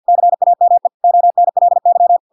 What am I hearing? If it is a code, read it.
Morse code, heard as a keyed tone, about 50 wpm, 700 Hz. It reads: CAME ONLY